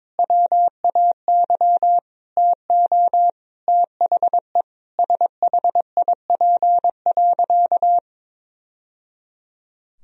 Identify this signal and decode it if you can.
Morse code, heard as a keyed tone, about 22 wpm, 700 Hz. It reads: WAY TO THE SHIP.
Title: Sentences from Top 1000 Words - Review Part 2 - 22wpm